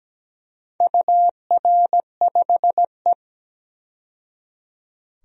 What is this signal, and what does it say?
Morse code, heard as a keyed tone, about 17 wpm, 700 Hz. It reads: UR5E
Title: DXCC-100 Call Signs - 4 Characters - 17wpm